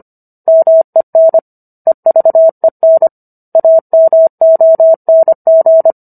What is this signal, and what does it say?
Morse code, heard as a keyed tone, about 25 wpm, 650 Hz. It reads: MEN EVEN AMONG